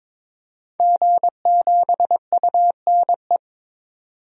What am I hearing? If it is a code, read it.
Morse code, heard as a keyed tone, about 22 wpm, 700 Hz. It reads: G7UNE
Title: DXCC-100 Call Signs - 5 Characters - 22wpm